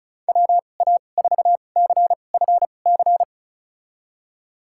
Morse code, heard as a keyed tone, about 36 wpm, spaced as for 25 wpm, 700 Hz. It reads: WA4CFC
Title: USA Call Signs - 6 Characters - 2x Character Spacing - 35wpm